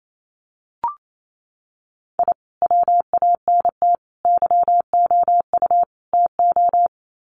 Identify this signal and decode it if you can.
Morse code, heard as a keyed tone, about 28 wpm, 700 Hz. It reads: I WANT YOU TO
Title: Sets of 4 Words 28wpm